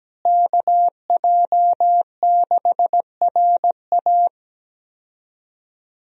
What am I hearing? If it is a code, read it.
Morse code, heard as a keyed tone, about 17 wpm, 700 Hz. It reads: KJ6RA
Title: USA Call Signs - 5 Characters - Repeated 3x - 17wpm